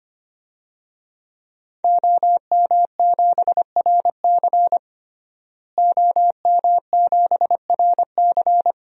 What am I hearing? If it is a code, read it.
Morse code, heard as a keyed tone, about 25 wpm, 700 Hz. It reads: OM7RC OM7RC
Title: DXCC-100 Call Signs - 5 Characters - Repeated 2x - 25wpm